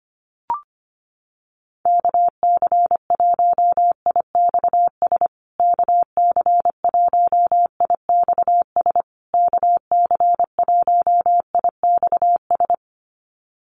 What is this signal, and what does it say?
Morse code, heard as a keyed tone, about 25 wpm, 700 Hz. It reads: KC1IXS KC1IXS KC1IXS
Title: USA Call Signs Repeated 3x 25wpm